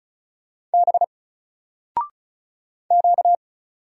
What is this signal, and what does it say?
Morse code, heard as a keyed tone, about 36 wpm, 700 Hz. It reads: B Q
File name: Single-Letter-Number-35wpm